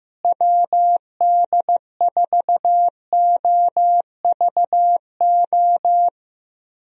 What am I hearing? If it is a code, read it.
Morse code, heard as a keyed tone, about 15 wpm, 700 Hz. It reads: WD4OVO
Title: USA Call Signs - 6 Characters - Repeated 2x - 15wpm